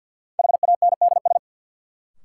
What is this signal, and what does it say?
Morse code, heard as a keyed tone, about 53 wpm, 700 Hz. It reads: HANDS
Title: Sets of 4 Words - New Word Drill - 50wpm